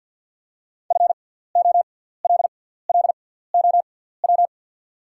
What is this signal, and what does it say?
Morse code, heard as a keyed tone, about 53 wpm, spaced as for 17 wpm, 690 Hz. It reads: FOLLOW
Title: 6x Character Spacing - Top 500 Words - 50wpm